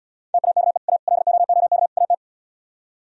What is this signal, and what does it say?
Morse code, heard as a keyed tone, about 78 wpm, 700 Hz. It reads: TAKE A LOOK AT